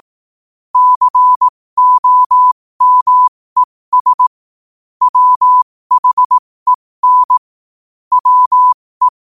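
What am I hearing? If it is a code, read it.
Morse code, heard as a keyed tone, about 18 wpm, spaced as for 15 wpm, 1000 Hz. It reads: COMES WHEN WE
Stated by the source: Morse practice sent at 15 wpm.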